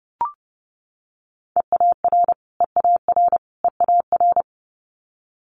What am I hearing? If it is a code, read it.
Morse code, heard as a keyed tone, about 30 wpm, 700 Hz. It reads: EAR EAR EAR
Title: Multi-letter Phonemes Repeated 3x 30wpm